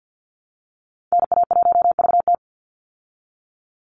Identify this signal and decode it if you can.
Morse code, heard as a keyed tone, about 49 wpm, 700 Hz. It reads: NA1VT
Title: Call Signs Repeated 2x - 50wpm